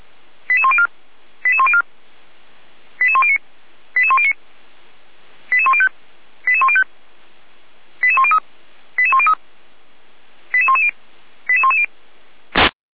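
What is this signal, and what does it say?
Five-tone selective call (ZVEI), heard as quick runs of five tones, tones of about 70 ms. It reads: 88104 88104 88109 88109 88105 88105 88103 88103 88110 88110
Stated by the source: ZVEI five-tone selective call received on an SDR, from the server of Feuerwehr Orlamünde (fire brigade).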